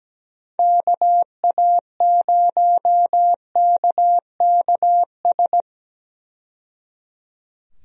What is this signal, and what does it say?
Morse code, heard as a keyed tone, about 17 wpm, 700 Hz. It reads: KA0KKS
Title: USA Call Signs - 6 Characters - 17wpm